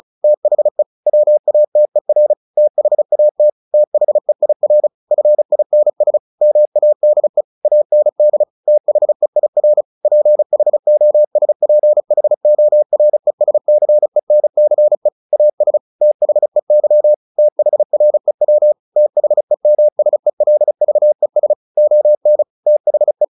Morse code, heard as a keyed tone, about 34 wpm, 600 Hz. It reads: THE WATER THAT THEIR FINS MADE AND THEIR PHOSPHORESCENCE AS THEY THREW THEMSELVES ON THE
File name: the old man and the sea 2 35wpm